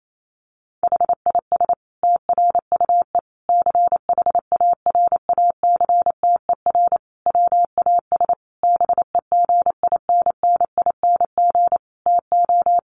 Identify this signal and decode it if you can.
Morse code, heard as a keyed tone, about 28 wpm, 700 Hz. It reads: HIS TRUE CHARACTER WAS BEGINNING TO
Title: Sentences from Top 1200 Words 28wpm